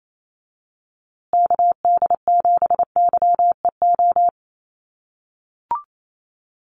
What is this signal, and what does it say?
Morse code, heard as a keyed tone, about 28 wpm, 700 Hz. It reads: KD7YEO
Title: USA Call Signs Repeated 2x 28wpm